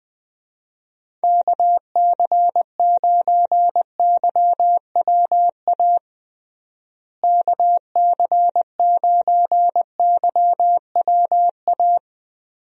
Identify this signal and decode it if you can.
Morse code, heard as a keyed tone, about 21 wpm, 700 Hz. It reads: KC9YWA KC9YWA
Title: USA Call Signs - 6 Characters - Repeated 2x - 20wpm